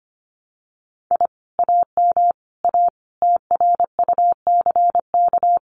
Morse code, heard as a keyed tone, about 25 wpm, 700 Hz. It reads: I AM A TRUCK
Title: Common QSO Phrases Encore 25wpm